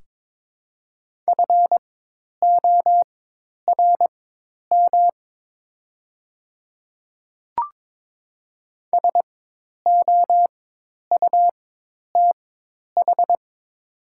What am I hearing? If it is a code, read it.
Morse code, heard as a keyed tone, about 22 wpm, spaced as for 10 wpm, 700 Hz. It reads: FORM SOUTH
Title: 4x Character Spacing - Top 500 Words - 22wpm